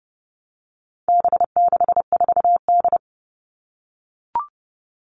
Morse code, heard as a keyed tone, about 30 wpm, 700 Hz. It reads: B64D
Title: All Cut Numbers - 4-Digit 30wpm